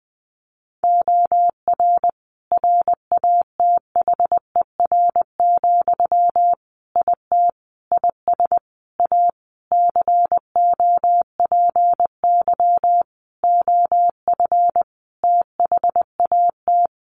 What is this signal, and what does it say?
Morse code, heard as a keyed tone, about 20 wpm, 700 Hz. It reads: OR RATHER, IT IS A COPY OF THAT
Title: Sentences from Top 1400 Words 20wpm